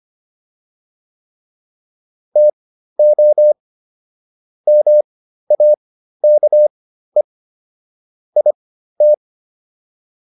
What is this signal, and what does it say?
Morse code, heard as a keyed tone, about 25 wpm, spaced as for 13 wpm, 600 Hz. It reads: TO MAKE IT